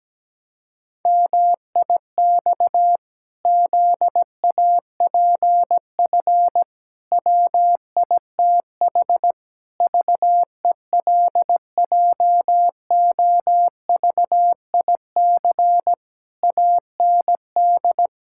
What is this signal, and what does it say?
Morse code, heard as a keyed tone, about 17 wpm, 700 Hz. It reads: MIX ZAPF WITH VELJOVIC AND
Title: Pangram Challenge v2 - 17wpm